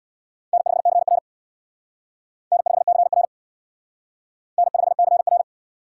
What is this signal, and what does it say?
Morse code, heard as a keyed tone, about 64 wpm, 700 Hz. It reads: N5CR N5CR N5CR